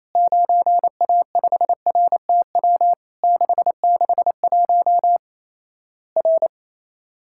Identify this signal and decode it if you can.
Morse code, heard as a keyed tone, about 27 wpm, 700 Hz. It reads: 9A5RTW 661 R
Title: Worked All Europe DX Contest - QTC Exchange - 28wpm